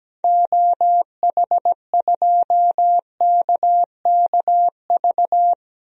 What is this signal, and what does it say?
Morse code, heard as a keyed tone, about 17 wpm, 700 Hz. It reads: OH2KKV
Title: DXCC-100 Call Signs - 6 Characters - Repeated 3x - 17wpm